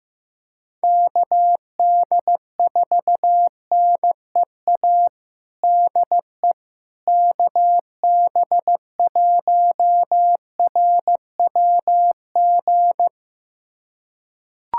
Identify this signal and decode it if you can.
Morse code, heard as a keyed tone, about 15 wpm, 700 Hz. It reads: KD4NEA DE KB1RWG
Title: USA Callsign DE Callsign - 6 Characters - 15wpm